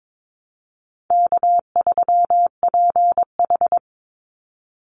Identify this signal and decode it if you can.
Morse code, heard as a keyed tone, about 22 wpm, 700 Hz. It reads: K3PH
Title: Call Signs Repeated 2x - 22wpm